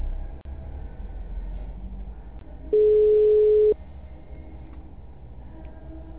Ringback tone (Europe, 425 Hz).